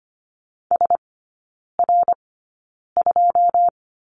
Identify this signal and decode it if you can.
Morse code, heard as a keyed tone, about 25 wpm, spaced as for 9 wpm, 690 Hz. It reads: SR2
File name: Lesson 15e - Number 5 - Three Characters 25wpm